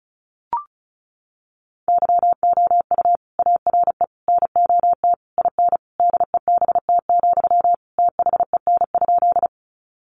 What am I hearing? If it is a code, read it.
Morse code, heard as a keyed tone, about 34 wpm, 700 Hz. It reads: YOU ARE NOT IN DEBT, THEN?
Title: Sentences from Top 1300 Words 35wpm